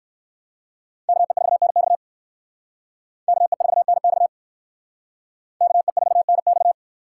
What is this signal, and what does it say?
Morse code, heard as a keyed tone, about 56 wpm, 700 Hz. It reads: KE4NX KE4NX KE4NX